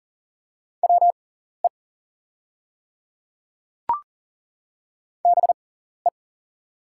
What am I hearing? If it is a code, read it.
Morse code, heard as a keyed tone, about 39 wpm, spaced as for 14 wpm, 690 Hz. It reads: WE BE
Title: 6x Character Spacing - Top 100 Words - 40wpm